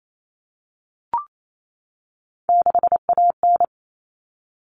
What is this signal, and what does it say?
Morse code, heard as a keyed tone, about 28 wpm, 700 Hz. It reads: 6AN
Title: Common Cut Numbers - 3-Digit 28wpm